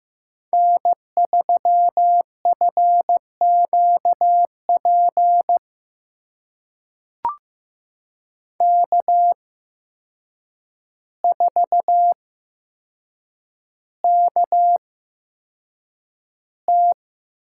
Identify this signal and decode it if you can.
Morse code, heard as a keyed tone, about 15 wpm, spaced as for 7 wpm, 700 Hz. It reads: N3FQP K4KT
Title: USA Call Signs - 5 Characters - 8x Character Spacing - 15wpm